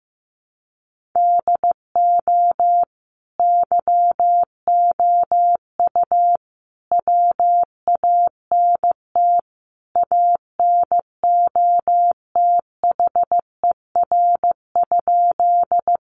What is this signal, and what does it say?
Morse code, heard as a keyed tone, about 15 wpm, 700 Hz. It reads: DO YOU WANT ANOTHER?
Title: Sentences from Top 600 Words 15wpm